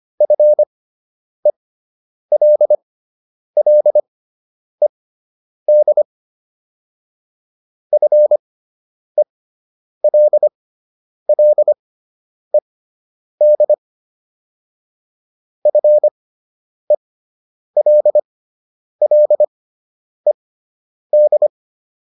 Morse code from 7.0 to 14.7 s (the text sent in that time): FELLED